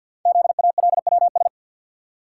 Morse code, heard as a keyed tone, about 48 wpm, 690 Hz. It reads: ZALWS
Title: Code Groups - 5 Letters - 50wpm